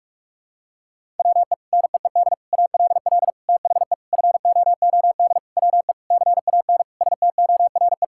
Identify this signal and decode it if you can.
Morse code, heard as a keyed tone, about 41 wpm, 700 Hz. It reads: WE NEED ALL THE FOOD WE CAN STORE
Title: Sentences from Top 1000 Words - Review Part 5 - 45wpm